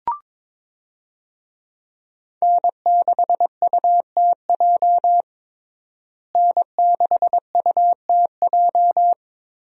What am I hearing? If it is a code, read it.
Morse code, heard as a keyed tone, about 21 wpm, 700 Hz. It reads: N6UTJ N6UTJ
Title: USA Call Signs - 5 Characters - Repeated 2x - 22wpm